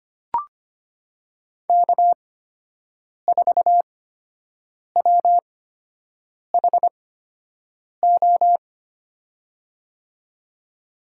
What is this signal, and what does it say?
Morse code, heard as a keyed tone, about 25 wpm, spaced as for 7 wpm, 700 Hz. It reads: K4WHO
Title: USA Call Signs - 5 Characters - 8x Character Spacing - 25wpm